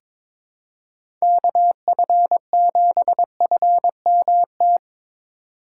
Morse code, heard as a keyed tone, about 21 wpm, 700 Hz. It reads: KF7FMT
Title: USA Call Signs - 6 Characters - 4x Character Spacing - 22wpm